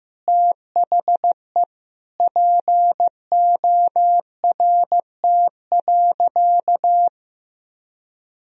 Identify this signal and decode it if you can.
Morse code, heard as a keyed tone, about 15 wpm, 700 Hz. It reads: THE PORT.
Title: Sentences from Top 1000 Words - Review Part 2 - 15wpm